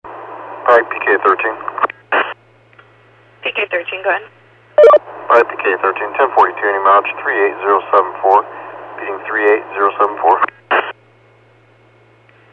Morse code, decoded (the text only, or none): none